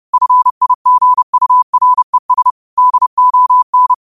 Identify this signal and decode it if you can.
Morse code, heard as a keyed tone, about 30 wpm, 1000 Hz. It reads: FIGURES DON
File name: qotdfiles / 30wpm